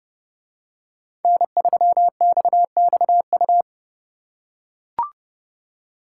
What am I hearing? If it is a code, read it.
Morse code, heard as a keyed tone, about 28 wpm, 700 Hz. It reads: N3XXU